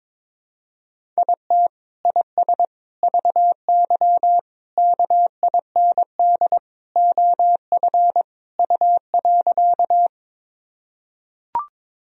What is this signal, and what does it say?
Morse code, heard as a keyed tone, about 21 wpm, 700 Hz. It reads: IT IS VY KIND OF U.
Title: Sentences from Top 200 Words and Abbreviations 22wpm